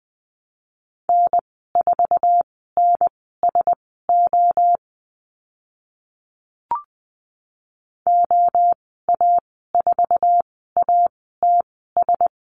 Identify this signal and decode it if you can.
Morse code, heard as a keyed tone, about 20 wpm, spaced as for 14 wpm, 700 Hz. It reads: N4NSO OA4ATS
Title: 082 Lesson 14g - Number 2 - Call Signs 20wpm